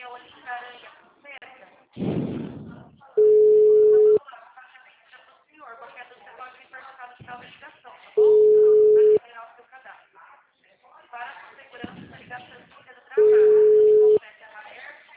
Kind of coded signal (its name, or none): telephone call progress tone